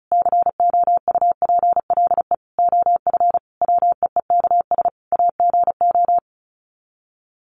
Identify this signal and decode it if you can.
Morse code, heard as a keyed tone, about 34 wpm, 700 Hz. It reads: COUPLE OF WEEKS AGO
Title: Sets of 5 Words 35wpm